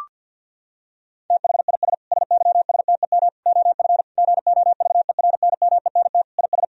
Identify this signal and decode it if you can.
Morse code, heard as a keyed tone, about 50 wpm, 700 Hz. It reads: THIS SYSTEM OF GOVERNMENT IS
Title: Sentences from Top 1000 Words - Review Part 3 - 50wpm